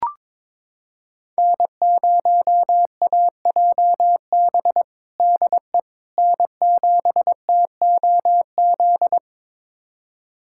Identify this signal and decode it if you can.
Morse code, heard as a keyed tone, about 22 wpm, 700 Hz. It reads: N0AJB DE N7TOZ
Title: USA Callsign DE Callsign - 5 Characters - 22wpm